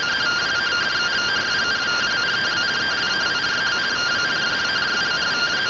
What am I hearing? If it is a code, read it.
Two-tone teleprinter signal, about 170 Hz shift.